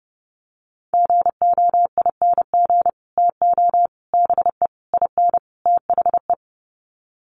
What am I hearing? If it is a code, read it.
Morse code, heard as a keyed tone, about 30 wpm, 700 Hz. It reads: GOING TO BE IN THE
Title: Sets of 5 Words Encore 30wpm